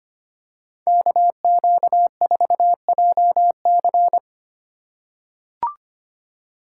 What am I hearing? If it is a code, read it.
Morse code, heard as a keyed tone, about 25 wpm, 700 Hz. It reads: KQ4JC
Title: USA Call Signs - 5 Characters - 4x Character Spacing - 25wpm